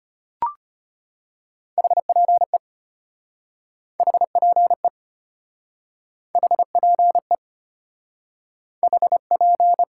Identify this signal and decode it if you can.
Morse code, heard as a keyed tone, about 33 wpm, 700 Hz. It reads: HPE HPE HPE HP